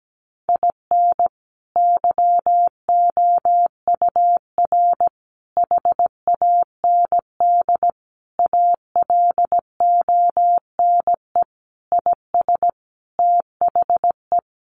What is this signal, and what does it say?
Morse code, heard as a keyed tone, about 17 wpm, 700 Hz. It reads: IN YOUR HAND ALONE IS THE
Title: Sentences from Top 1400 Words 17wpm